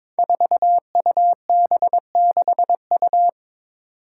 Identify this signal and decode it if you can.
Morse code, heard as a keyed tone, about 22 wpm, 700 Hz. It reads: 4UB6U
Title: Five Challenging Letter-Number Combinations 22wpm